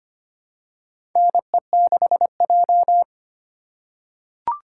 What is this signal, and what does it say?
Morse code, heard as a keyed tone, about 25 wpm, 700 Hz. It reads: NE6J
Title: USA Call Signs - 4 Characters - 8x Character Spacing - 25wpm